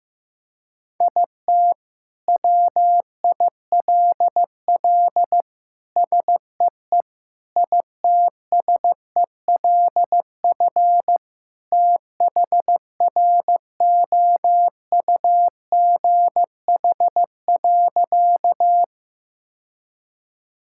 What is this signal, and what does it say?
Morse code, heard as a keyed tone, about 15 wpm, 700 Hz. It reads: IT WILL SEE ITSELF THROUGH.
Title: Sentences from Top 1000 Words - Review Part 5 - 15wpm